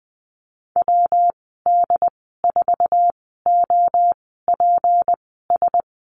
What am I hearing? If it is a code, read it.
Morse code, heard as a keyed tone, about 20 wpm, spaced as for 15 wpm, 700 Hz. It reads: WD4OPS
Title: 193 Lesson 31g - Letter v - Call Signs 20wpm